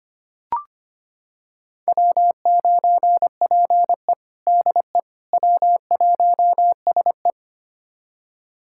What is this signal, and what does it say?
Morse code, heard as a keyed tone, about 25 wpm, 700 Hz. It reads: W9PE DE W1SE
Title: USA Callsign DE Callsign - 4 Characters - 25wpm